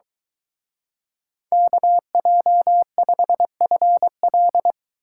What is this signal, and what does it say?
Morse code, heard as a keyed tone, about 23 wpm, 700 Hz. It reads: KJ5FL